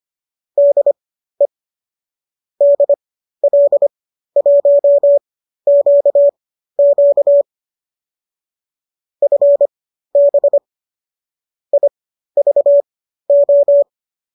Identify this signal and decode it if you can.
Morse code, heard as a keyed tone, about 25 wpm, spaced as for 13 wpm, 570 Hz. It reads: DE DL1QQ FB IVO
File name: qso219 13